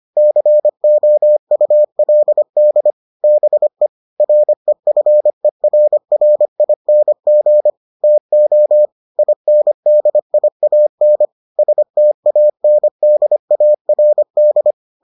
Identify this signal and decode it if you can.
Morse code, heard as a keyed tone, about 25 wpm, 600 Hz. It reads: COULD BE REFERRING TO INDIAN STANDARD